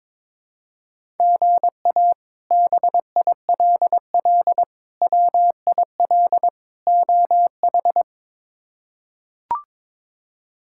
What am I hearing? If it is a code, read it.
Morse code, heard as a keyed tone, about 22 wpm, 700 Hz. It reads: GA BILL WIL OH
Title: K1USN SST QSO Practice - 22wpm